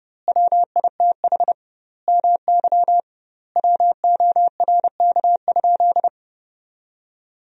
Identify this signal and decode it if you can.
Morse code, heard as a keyed tone, about 30 wpm, 700 Hz. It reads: WITH MY WORK?
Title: 2x Word-Spacing - Sentences from Top 500 Words - 30wpm